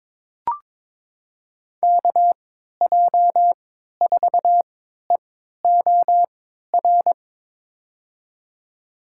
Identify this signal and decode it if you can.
Morse code, heard as a keyed tone, about 22 wpm, spaced as for 12 wpm, 700 Hz. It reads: KJ4EOR
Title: USA Call Signs - 6 Characters - 3x Character Spacing - 22wpm